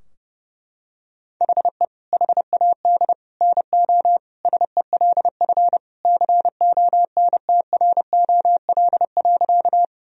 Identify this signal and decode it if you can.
Morse code, heard as a keyed tone, about 30 wpm, 700 Hz. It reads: HE HAD NO SELF CONTROL.
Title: Sentences from Top 1000 Words - Review Part 3 - 30wpm